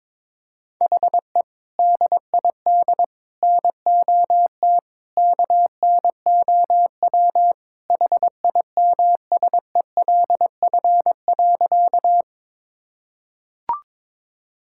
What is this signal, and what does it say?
Morse code, heard as a keyed tone, about 22 wpm, 700 Hz. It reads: HE DID NOT KNOW HIMSELF.
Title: Sentences from Top 1600 Words 22wpm